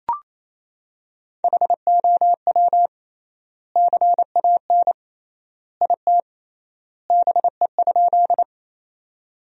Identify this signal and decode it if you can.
Morse code, heard as a keyed tone, about 28 wpm, 700 Hz. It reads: HOW CAN IT BE?
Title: 3x Word-Spacing - Sentences from Top 100 Words - 28wpm